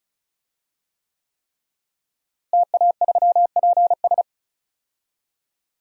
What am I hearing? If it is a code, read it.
Morse code, heard as a keyed tone, about 34 wpm, 700 Hz. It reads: TA3PS
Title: DXCC-100 Call Signs - 5 Characters - Repeated 3x - 35wpm